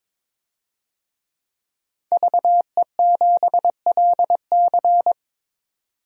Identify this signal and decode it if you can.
Morse code, heard as a keyed tone, about 22 wpm, 700 Hz. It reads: VE7LC
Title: DXCC-100 Call Signs - 5 Characters - 22wpm